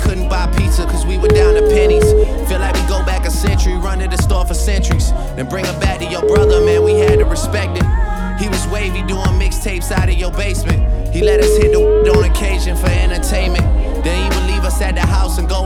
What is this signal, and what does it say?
Ringback tone (Europe, 425 Hz), with noise.